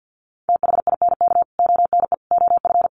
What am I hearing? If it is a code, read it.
Morse code, heard as a keyed tone, about 50 wpm, 700 Hz. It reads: THINK ONE OF
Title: Sets of 5 Words Encore2 50wpm